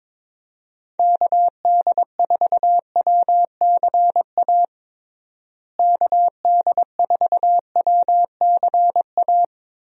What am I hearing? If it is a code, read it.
Morse code, heard as a keyed tone, about 22 wpm, 700 Hz. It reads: KD4WCA KD4WCA
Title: USA Call Signs - 6 Characters - Repeated 2x - 22wpm